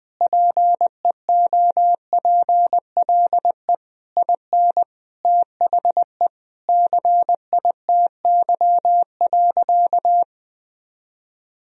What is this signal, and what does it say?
Morse code, heard as a keyed tone, about 20 wpm, 700 Hz. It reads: PEOPLE IN THE CITY.
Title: Sentences from Top 1000 Words - Review Part 1 - 20wpm